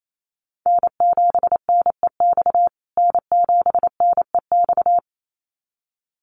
Morse code, heard as a keyed tone, about 28 wpm, 700 Hz. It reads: N7NEX N7NEX
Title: USA Call Signs Repeated 2x 28wpm